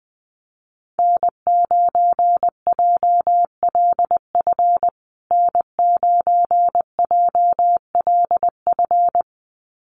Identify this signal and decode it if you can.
Morse code, heard as a keyed tone, about 20 wpm, 700 Hz. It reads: N9JLF N9JLF
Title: USA Call Signs Repeated 2x 20wpm